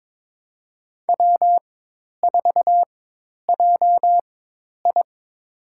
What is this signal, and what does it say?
Morse code, heard as a keyed tone, about 21 wpm, spaced as for 10 wpm, 700 Hz. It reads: W4JI
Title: USA Call Signs - 4 Characters - 4x Character Spacing - 22wpm